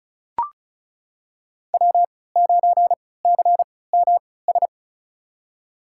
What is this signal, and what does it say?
Morse code, heard as a keyed tone, about 36 wpm, spaced as for 20 wpm, 700 Hz. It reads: W9CMS
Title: USA Call Signs - 5 Characters - 3x Character Spacing - 35wpm